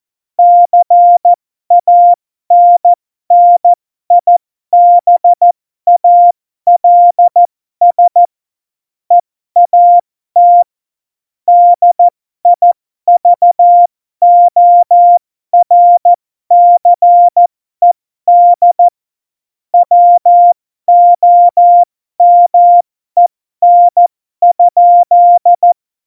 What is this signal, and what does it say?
Morse code, heard as a keyed tone, about 14 wpm, spaced as for 12 wpm, 700 Hz. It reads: CANNIBALS EAT DIVORCED WOMEN?
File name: Morsekurs / Witze